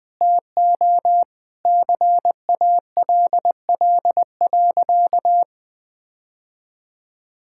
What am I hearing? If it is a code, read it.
Morse code, heard as a keyed tone, about 20 wpm, 700 Hz. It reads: TO CALL.
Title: Sentences with 2-Word Phrases - 20wpm